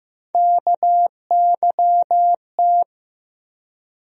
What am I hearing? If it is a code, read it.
Morse code, heard as a keyed tone, about 16 wpm, 700 Hz. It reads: KYT
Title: USA Call Sign Postfixes - 3 Letters 15wpm